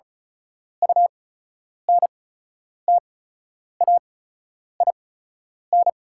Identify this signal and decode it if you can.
Morse code, heard as a keyed tone, about 36 wpm, spaced as for 10 wpm, 700 Hz. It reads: UNTAIN
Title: 8x Character Spacing - Top 300 Words - 35wpm